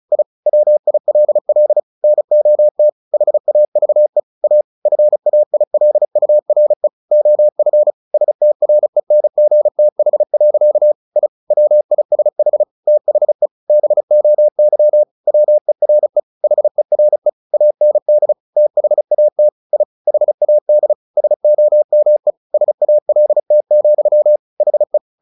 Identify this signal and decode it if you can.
Morse code, heard as a keyed tone, about 34 wpm, 600 Hz. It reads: I WILL NOT HAVE A FAILURE OF STRENGTH. I WISH THE BOY WERE HERE AND THAT I HAD SOME SALT, HE
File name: the old man and the sea / the old man and the sea 1 35wpm